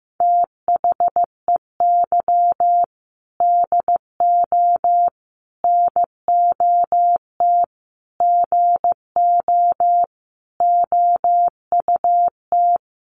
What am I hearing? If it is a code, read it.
Morse code, heard as a keyed tone, about 15 wpm, 700 Hz. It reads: THEY DO NOT GO OUT
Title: Sentences from Top 700 Words 15wpm